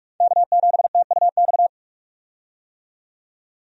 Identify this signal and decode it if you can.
Morse code, heard as a keyed tone, about 45 wpm, 700 Hz. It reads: K7TUX
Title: USA Call Signs - 5 Characters - 45wpm